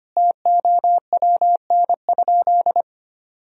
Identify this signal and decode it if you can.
Morse code, heard as a keyed tone, about 25 wpm, 700 Hz. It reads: TOWN?
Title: Sentences from Top 1000 Words - Review Part 4 - 25wpm